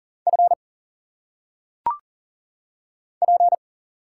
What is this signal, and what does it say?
Morse code, heard as a keyed tone, about 38 wpm, 700 Hz. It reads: F P
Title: Single Letters - Phonetic Pronunciation 40wpm